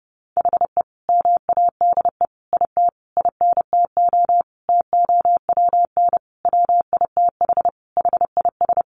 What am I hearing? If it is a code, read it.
Morse code, heard as a keyed tone, about 30 wpm, 700 Hz. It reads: HE MADE IT INTO TOWN WITH HIS
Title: Sentences from Top 1300 Words 30wpm